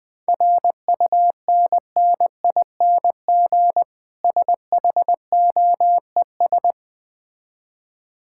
Morse code, heard as a keyed tone, about 20 wpm, 700 Hz. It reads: RUNNING SHOES
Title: Difficult Words with Hints - 20wpm